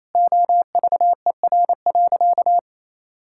Morse code, heard as a keyed tone, about 28 wpm, 700 Hz. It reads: OVER.
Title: Sentences from Top 1600 Words 28wpm